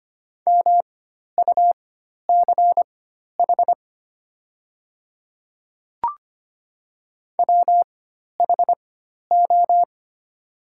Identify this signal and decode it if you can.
Morse code, heard as a keyed tone, about 25 wpm, spaced as for 12 wpm, 700 Hz. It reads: MUCH WHO
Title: 4x Character Spacing - Top 300 Words - 25wpm